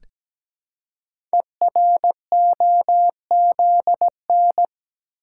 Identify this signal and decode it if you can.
Morse code, heard as a keyed tone, about 17 wpm, 700 Hz. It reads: EROZN
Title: Code Groups - 5 Letters - 17wpm